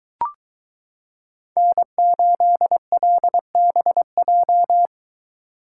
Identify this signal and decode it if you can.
Morse code, heard as a keyed tone, about 22 wpm, 700 Hz. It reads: N8LBJ